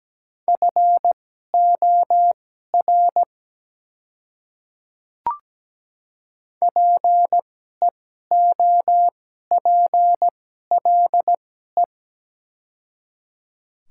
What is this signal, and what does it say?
Morse code, heard as a keyed tone, about 17 wpm, spaced as for 12 wpm, 690 Hz. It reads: FOR PEOPLE
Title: 2x Character Spacing - Top 100 Words - 17wpm